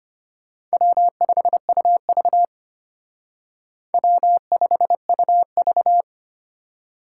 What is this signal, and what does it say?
Morse code, heard as a keyed tone, about 32 wpm, 700 Hz. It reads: W5UV W5UV